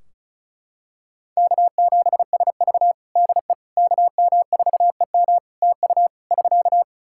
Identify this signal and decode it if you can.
Morse code, heard as a keyed tone, about 36 wpm, 700 Hz. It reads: K7SV DE KM4EM TU <SK>
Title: Practice QSO - Exchange 4 - Ending - 35wpm